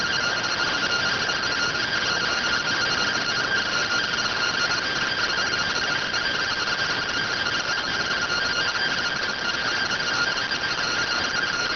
Two-tone teleprinter signal, about 170 Hz shift.